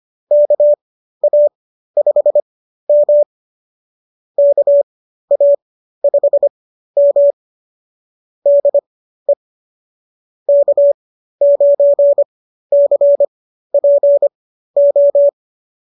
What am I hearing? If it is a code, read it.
Morse code, heard as a keyed tone, about 25 wpm, spaced as for 13 wpm, 570 Hz. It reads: KA5M KA5M DE K9CPO
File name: qso211 13